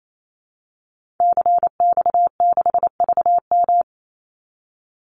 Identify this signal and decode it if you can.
Morse code, heard as a keyed tone, about 28 wpm, 700 Hz. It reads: CX6VM
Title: Call Signs Repeated 2x - 28wpm